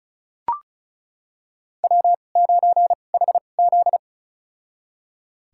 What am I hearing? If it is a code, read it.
Morse code, heard as a keyed tone, about 35 wpm, spaced as for 25 wpm, 700 Hz. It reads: W9HZ
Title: USA Call Signs - 4 Characters - 2x Character Spacing - 35wpm